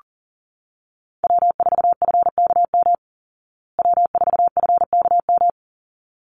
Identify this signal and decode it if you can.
Morse code, heard as a keyed tone, about 40 wpm, 700 Hz. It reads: W4FKM W4FKM
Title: Call Signs Repeated 3x - 40wpm